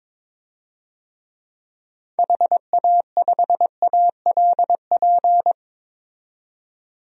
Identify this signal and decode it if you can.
Morse code, heard as a keyed tone, about 22 wpm, 700 Hz. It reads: HA5ALP
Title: DXCC-100 Call Signs - 6 Characters - 22wpm